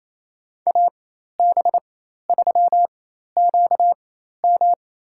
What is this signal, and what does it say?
Morse code, heard as a keyed tone, about 28 wpm, spaced as for 13 wpm, 700 Hz. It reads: AB3QM
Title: USA Call Signs - 5 Characters - 4x Character Spacing - 28wpm